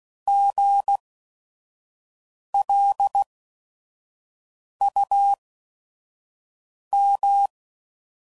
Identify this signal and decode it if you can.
Morse code, heard as a keyed tone, about 16 wpm, spaced as for 5 wpm, 800 Hz. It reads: GLUM
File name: Prefixes / By First Letter / Gs